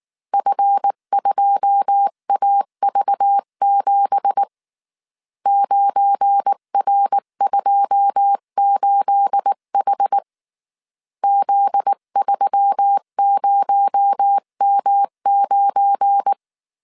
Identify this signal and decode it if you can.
Morse code, heard as a keyed tone, about 19 wpm, 800 Hz. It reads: F2AV7 9R28H Z30M9
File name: CW / DET / 1 9 1500C S B 18wpm 2551